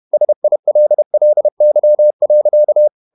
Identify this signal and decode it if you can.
Morse code, heard as a keyed tone, about 31 wpm, 600 Hz. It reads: SILLY.
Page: Never 31wpm | CW med Gnister